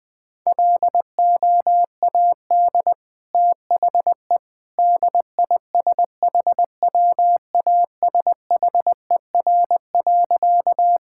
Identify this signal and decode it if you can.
Morse code, heard as a keyed tone, about 21 wpm, 700 Hz. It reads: LOAD THE DISHWASHER.
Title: Easy Sentences with Hints - 20wpm